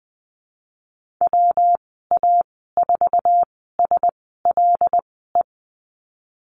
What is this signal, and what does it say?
Morse code, heard as a keyed tone, about 20 wpm, spaced as for 14 wpm, 700 Hz. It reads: WA4SLE
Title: 161 Lesson 26g - Letter g - Call Signs 20wpm